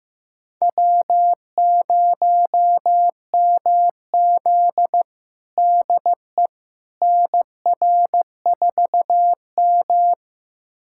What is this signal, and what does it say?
Morse code, heard as a keyed tone, about 15 wpm, 700 Hz. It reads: W0MZ DE NR4M
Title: USA Callsign DE Callsign - 4 Characters - 15wpm